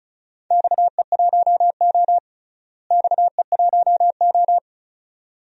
Morse code, heard as a keyed tone, about 36 wpm, 700 Hz. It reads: XE1O XE1O
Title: DXCC-100 Call Signs - 4 Characters - Repeated 2x - 35wpm